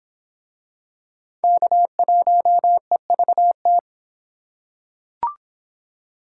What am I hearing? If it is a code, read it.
Morse code, heard as a keyed tone, about 26 wpm, 700 Hz. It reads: K1EVT